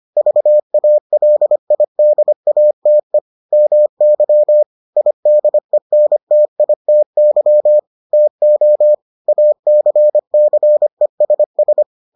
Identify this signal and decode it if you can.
Morse code, heard as a keyed tone, about 25 wpm, 600 Hz. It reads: VALIDATE MY IDENTITY TO ACCESS